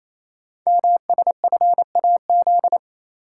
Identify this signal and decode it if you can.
Morse code, heard as a keyed tone, about 29 wpm, 700 Hz. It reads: MSFAZ
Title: Code Groups - 5 Letters - 28wpm